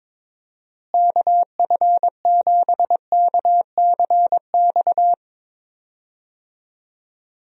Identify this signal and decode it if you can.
Morse code, heard as a keyed tone, about 22 wpm, 700 Hz. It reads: KF7KCX
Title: USA Call Signs - 6 Characters - Repeated 3x - 22wpm